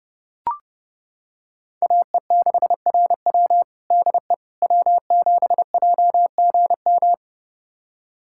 Morse code, heard as a keyed tone, about 30 wpm, 700 Hz. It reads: AE6RW DE W7JGM
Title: USA Callsign DE Callsign - 5 Characters - 30wpm